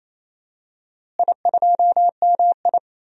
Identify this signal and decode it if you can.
Morse code, heard as a keyed tone, about 28 wpm, 700 Hz. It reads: I2MI
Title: DXCC-100 Call Signs - 4 Characters - Repeated 2x - 28wpm